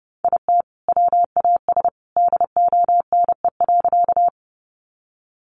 Morse code, heard as a keyed tone, about 30 wpm, 700 Hz. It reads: IT WAS DONE.
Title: Sentences from Top 500 Words 30wpm